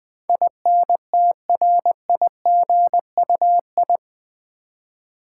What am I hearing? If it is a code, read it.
Morse code, heard as a keyed tone, about 20 wpm, 700 Hz. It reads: INTRIGUI
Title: Build A Word - 4 Letters At A Time - 10-Letter Words - 20wpm